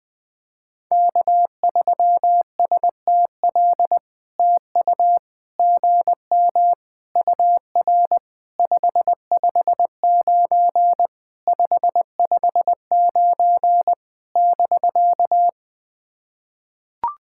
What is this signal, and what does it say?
Morse code, heard as a keyed tone, about 20 wpm, 700 Hz. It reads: K3STL TU GM UR 559 559 <BK>
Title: POTA QSO Practice - 20wpm